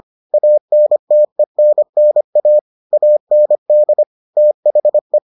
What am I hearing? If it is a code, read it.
Morse code, heard as a keyed tone, about 25 wpm, 600 Hz. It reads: ANTENNA AND THE